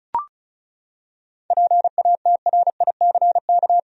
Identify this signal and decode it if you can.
Morse code, heard as a keyed tone, about 34 wpm, 700 Hz. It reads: PATRICK
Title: USA Common Names - 35wpm